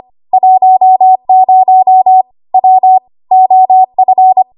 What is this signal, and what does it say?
Morse code, heard as a keyed tone, about 25 wpm, 750 Hz. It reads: 10 W OF
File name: Morse / 25WPM